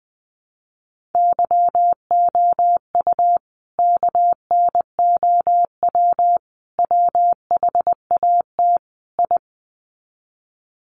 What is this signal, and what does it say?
Morse code, heard as a keyed tone, about 20 wpm, 700 Hz. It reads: YOU KNOW WHAT I
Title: Sets of 4 Words 20wpm